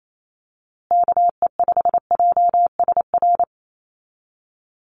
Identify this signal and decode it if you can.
Morse code, heard as a keyed tone, about 28 wpm, 700 Hz. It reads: KE5JSR
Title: USA Call Signs Repeated 3x 28wpm